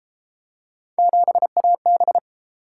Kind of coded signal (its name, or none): Morse code